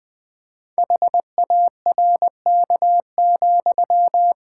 Morse code, heard as a keyed tone, about 20 wpm, 700 Hz. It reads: HARK,
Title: Pangram Challenge v2 - 20wpm